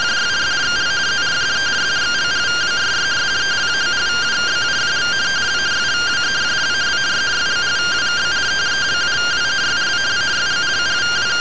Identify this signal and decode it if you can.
Two-tone teleprinter signal, about 170 Hz shift.